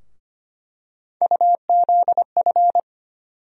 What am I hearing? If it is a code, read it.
Morse code, heard as a keyed tone, about 25 wpm, 700 Hz. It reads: UZF
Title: USA Call Sign Postfixes - 3 Letters 25wpm